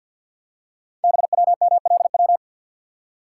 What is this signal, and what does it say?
Morse code, heard as a keyed tone, about 50 wpm, 700 Hz. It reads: BWMLW
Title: Code Groups - 5 Letters - 50wpm